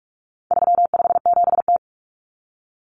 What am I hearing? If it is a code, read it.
Morse code, heard as a keyed tone, about 44 wpm, 700 Hz. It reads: 357T
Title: Common Cut Numbers - 4-Digit 45wpm